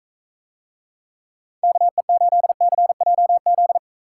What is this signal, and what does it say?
Morse code, heard as a keyed tone, about 43 wpm, 700 Hz. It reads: KE8CJZ